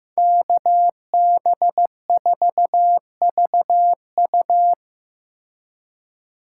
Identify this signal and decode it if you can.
Morse code, heard as a keyed tone, about 15 wpm, 700 Hz. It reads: KB4VU
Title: K1USN SST QSO Practice - 15wpm